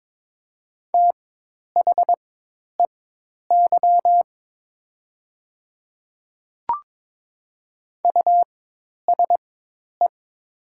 Morse code, heard as a keyed tone, about 22 wpm, spaced as for 10 wpm, 700 Hz. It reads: THEY USE
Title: 4x Character Spacing - Top 200 Words - 22wpm